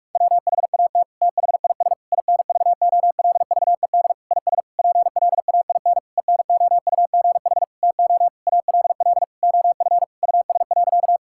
Morse code, heard as a keyed tone, about 42 wpm, 700 Hz. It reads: WHAT THIS INVOLVED IS PLAIN ENOUGH TO ALL OF US.
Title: Sentences from Top 1000 Words - Review Part 5 - 45wpm